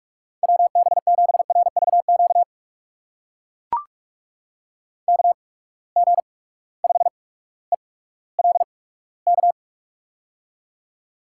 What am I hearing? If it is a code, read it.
Morse code, heard as a keyed tone, about 44 wpm, spaced as for 19 wpm, 700 Hz. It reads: WB7RVQ KG5ELK
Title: USA Call Signs - 6 Characters - 8x Character Spacing - 45wpm